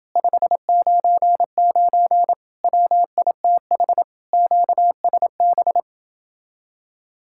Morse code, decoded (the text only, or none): 599 WITH QSB